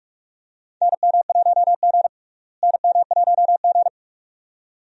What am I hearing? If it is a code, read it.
Morse code, heard as a keyed tone, about 46 wpm, 700 Hz. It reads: NM1G NM1G
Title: USA Call Signs - 4 Characters - Repeated 2x - 45wpm